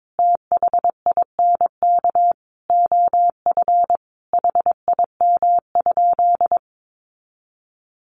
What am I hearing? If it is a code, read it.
Morse code, heard as a keyed tone, about 22 wpm, 700 Hz. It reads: THINK OF HIM?
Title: Sentences from Top 200 Words 22wpm